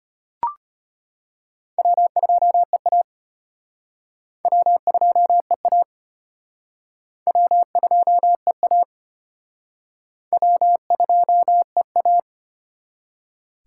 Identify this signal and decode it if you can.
Morse code, heard as a keyed tone, about 31 wpm, 700 Hz. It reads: W2EA W2EA W2EA W2EA
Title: Speed Racing - USA 1x2 Call Signs 25wpm